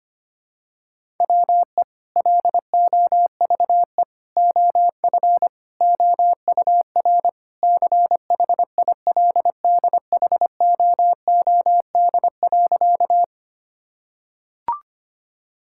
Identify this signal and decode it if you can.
Morse code, heard as a keyed tone, about 25 wpm, 700 Hz. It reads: WE LOVE OF OUR CHILDHOOD.
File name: practice / 25wpm / Sentences with 2-Word Phrases - 25wpm